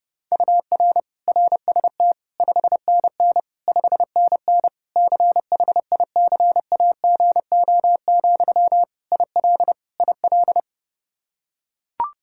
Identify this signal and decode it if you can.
Morse code, heard as a keyed tone, about 30 wpm, 700 Hz. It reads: UR RST 5NN 5NN CHICAGO, IL IL
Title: K1USN SST QSO Practice 30wpm